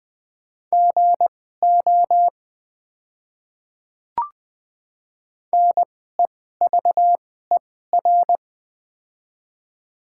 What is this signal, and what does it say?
Morse code, heard as a keyed tone, about 21 wpm, spaced as for 15 wpm, 700 Hz. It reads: GO NEVER
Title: 2x Character Spacing - Top 300 Words - 20wpm